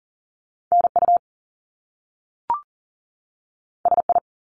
Morse code, heard as a keyed tone, about 39 wpm, 700 Hz. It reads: NU SI
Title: Top Two Letter Combinations in Words 40wpm